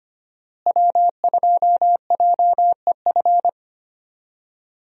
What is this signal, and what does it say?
Morse code, heard as a keyed tone, about 25 wpm, 700 Hz. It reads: W2JEF
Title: USA Call Signs - 5 Characters - Repeated 2x - 25wpm